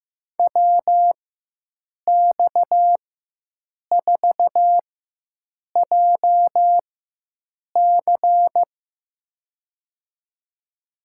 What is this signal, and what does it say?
Morse code, heard as a keyed tone, about 16 wpm, spaced as for 7 wpm, 700 Hz. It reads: WX4JC
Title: USA Call Signs - 5 Characters - 4x Character Spacing - 15wpm